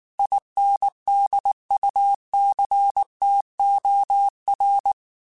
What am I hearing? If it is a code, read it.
Morse code, heard as a keyed tone, about 19 wpm, 800 Hz. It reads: INDUCTOR